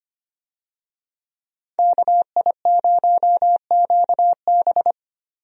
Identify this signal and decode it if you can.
Morse code, heard as a keyed tone, about 25 wpm, 700 Hz. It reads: KI0QB
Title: USA Call Signs - 5 Characters - 25wpm